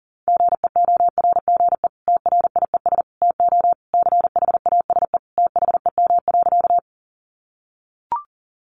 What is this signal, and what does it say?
Morse code, heard as a keyed tone, about 40 wpm, 700 Hz. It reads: GEORGE TRIES TO CHASE THEM.
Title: Additional Easy Phrases 40wpm